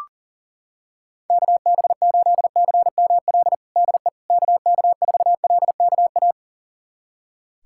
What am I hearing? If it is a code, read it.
Morse code, heard as a keyed tone, about 37 wpm, 700 Hz. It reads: KB8CML DE KK4LKA
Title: USA Callsign DE Callsign 40wpm